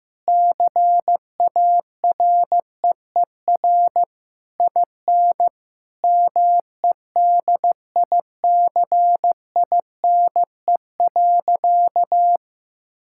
Morse code, heard as a keyed tone, about 15 wpm, 700 Hz. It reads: CAREER IN MEDICINE.
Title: Sentences from Top 1000 Words - Review Part 6 - 15wpm